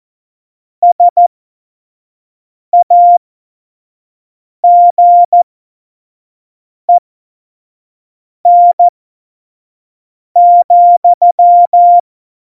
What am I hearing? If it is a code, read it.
Morse code, heard as a keyed tone, about 14 wpm, spaced as for 5 wpm, 700 Hz. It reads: SAGEN,
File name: Morsekurs / Witze / Witz 16. April 2019, 25 F70